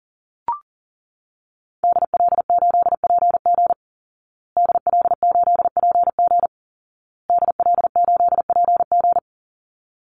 Morse code, heard as a keyed tone, about 40 wpm, 700 Hz. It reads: DL8PG DL8PG DL8PG
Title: Call Signs Repeated 2x - 40wpm